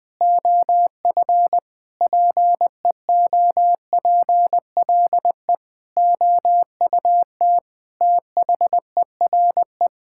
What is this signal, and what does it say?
Morse code, heard as a keyed tone, about 21 wpm, 700 Hz. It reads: OF PEOPLE OUT THERE
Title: Sentences with 9-Word Phrases - Spoken Beginning - 20wpm